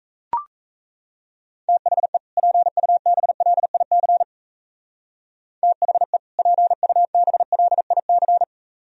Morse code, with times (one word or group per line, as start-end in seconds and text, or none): 1.7-2.2 THE
2.4-4.2 PUBLIC
5.6-6.2 THE
6.4-8.4 PUBLIC